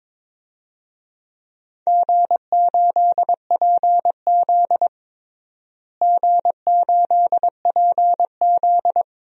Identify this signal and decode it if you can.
Morse code, heard as a keyed tone, about 22 wpm, 700 Hz. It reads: G8PZ G8PZ
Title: DXCC-100 Call Signs - 4 Characters - Repeated 3x - 22wpm